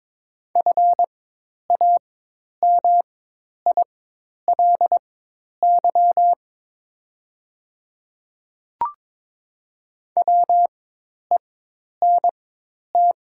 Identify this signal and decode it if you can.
Morse code, heard as a keyed tone, about 22 wpm, spaced as for 10 wpm, 690 Hz. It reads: FAMILY WENT
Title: 4x Character Spacing - Top 500 Words - 22wpm